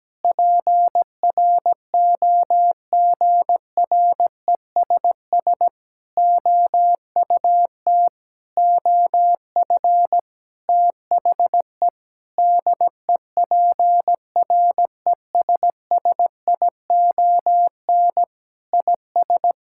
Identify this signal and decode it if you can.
Morse code, heard as a keyed tone, about 17 wpm, 700 Hz. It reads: PROGRESS OUT OF THE DEPRESSION IS
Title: Sentences from Top 1600 Words 17wpm